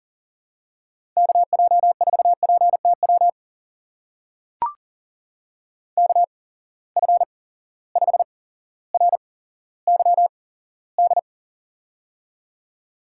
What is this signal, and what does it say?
Morse code, heard as a keyed tone, about 39 wpm, spaced as for 17 wpm, 700 Hz. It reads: KJ4PTW KF5RYD
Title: USA Call Signs - 6 Characters - 8x Character Spacing - 40wpm